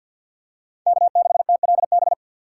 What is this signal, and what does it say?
Morse code, heard as a keyed tone, about 52 wpm, 690 Hz. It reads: K6TLB
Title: USA Call Signs - 2x Character Spacing - 50wpm